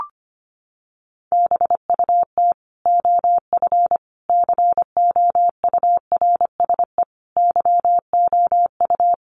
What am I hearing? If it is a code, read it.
Morse code, heard as a keyed tone, about 25 wpm, 700 Hz. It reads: BUT OF COURSE YOU
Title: Sentences from Top 1500 Words 25wpm